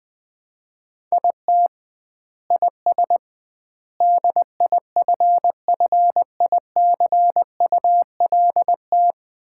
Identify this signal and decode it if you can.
Morse code, heard as a keyed tone, about 20 wpm, 700 Hz. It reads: IT IS DIFFICULT
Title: 2x Word-Spacing - Sets of 4 Words - 20wpm